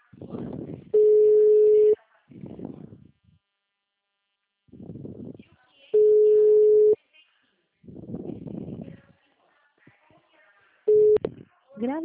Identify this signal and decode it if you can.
Ringback tone (Europe, 425 Hz), with speech.
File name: out- -2002- - - .52629